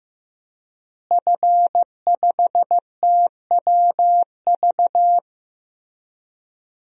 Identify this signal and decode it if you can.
Morse code, heard as a keyed tone, about 15 wpm, 700 Hz. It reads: F5TWV
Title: DXCC-100 Call Signs - 5 Characters - Repeated 3x - 15wpm